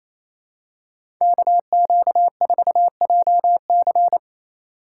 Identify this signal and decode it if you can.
Morse code, heard as a keyed tone, about 28 wpm, 700 Hz. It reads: KQ4JC
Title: USA Call Signs - 5 Characters - 2x Character Spacing - 28wpm